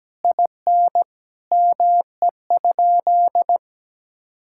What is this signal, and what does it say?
Morse code, heard as a keyed tone, about 17 wpm, 700 Hz. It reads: IN ME?
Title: Sentences from Top 1000 Words - Review Part 6 - 17wpm